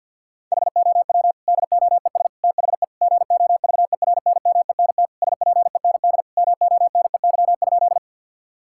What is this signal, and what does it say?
Morse code, heard as a keyed tone, about 47 wpm, 700 Hz. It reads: HOW DOES THE GOVERNMENT SPEND MONEY?
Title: Sentences from Top 1000 Words - Review Part 5 - 50wpm